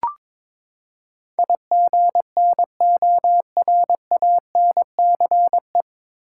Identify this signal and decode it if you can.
Morse code, heard as a keyed tone, about 22 wpm, 700 Hz. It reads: IGNORANCE
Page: Morse Code Ninja practice 22wpm